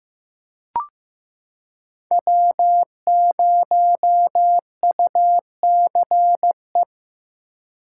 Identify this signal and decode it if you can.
Morse code, heard as a keyed tone, about 15 wpm, 700 Hz. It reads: W0UCE
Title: K1USN SST QSO Practice - 15wpm